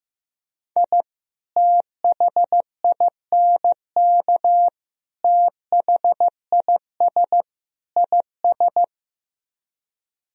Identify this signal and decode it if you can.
Morse code, heard as a keyed tone, about 15 wpm, 700 Hz. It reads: I THINK THIS IS
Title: Speed-Racing - Sets of 4 Words 15wpm